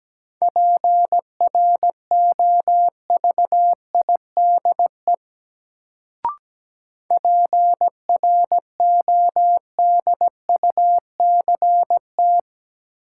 Morse code, heard as a keyed tone, about 17 wpm, 700 Hz. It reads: PROVIDE PRODUCT
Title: Sentences from Top 1000 Words - Review Part 2 - New Word Drill - 17wpm